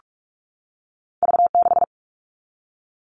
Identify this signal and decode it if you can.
Morse code, heard as a keyed tone, about 46 wpm, 700 Hz. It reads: V6
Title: All Cut Numbers - 2-Digit 45wpm